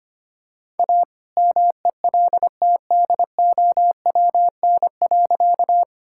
Morse code, heard as a keyed tone, about 25 wpm, 700 Hz. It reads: A MELTDOWN.
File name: practice / 25wpm / All-Age Appropriate Jokes - 25wpm